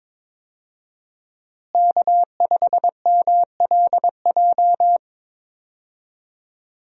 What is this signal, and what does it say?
Morse code, heard as a keyed tone, about 22 wpm, 700 Hz. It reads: K5MLJ
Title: USA Call Signs - 5 Characters - Repeated 2x - 22wpm